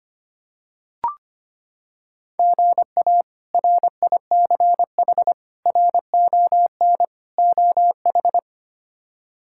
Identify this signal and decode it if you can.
Morse code, heard as a keyed tone, about 25 wpm, 700 Hz. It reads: GA RICH RON OH
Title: K1USN SST QSO Practice - 25wpm